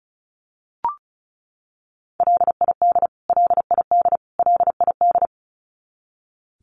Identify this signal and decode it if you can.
Morse code, heard as a keyed tone, about 34 wpm, 700 Hz. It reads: LID LID LID
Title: Abbreviations Repeated 3x - 35wpm